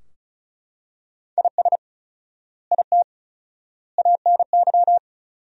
Morse code, heard as a keyed tone, about 36 wpm, 700 Hz. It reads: IS IT ANY
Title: 4x Word-Spacing - Sentences from Top 200 Words - 35wpm